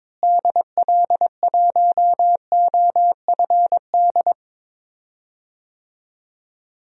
Morse code, heard as a keyed tone, about 22 wpm, 700 Hz. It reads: DL1OFD
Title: DXCC-100 Call Signs - 6 Characters - Repeated 3x - 22wpm